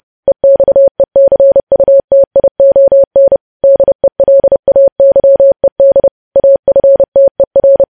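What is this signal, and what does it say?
Morse code, heard as a keyed tone, about 30 wpm, 550 Hz. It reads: EXECUTION DELAYED AFTER
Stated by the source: News headlines sent in Morse at 30 wpm, preceded by VVV.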